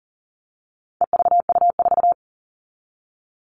Morse code, heard as a keyed tone, about 40 wpm, 700 Hz. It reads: EVU4
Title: All Cut Numbers - 4-Digit 40wpm